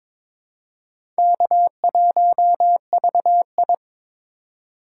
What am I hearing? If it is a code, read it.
Morse code, heard as a keyed tone, about 22 wpm, 700 Hz. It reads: K1VI
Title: USA Call Signs - 4 Characters - 2x Character Spacing - 22wpm